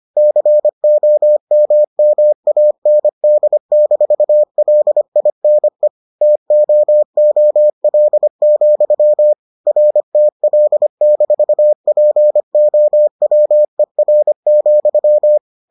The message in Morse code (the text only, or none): COMMAND-LINE TOOL, RTL-POWER,